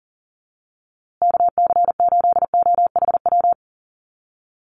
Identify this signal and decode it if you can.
Morse code, heard as a keyed tone, about 37 wpm, 700 Hz. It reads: KC8OHW
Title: USA Call Signs Repeated 3x 40wpm